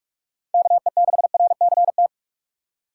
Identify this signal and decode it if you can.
Morse code, heard as a keyed tone, about 46 wpm, 700 Hz. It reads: KE6RCT
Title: USA Call Signs - 3x Character Spacing - 45wpm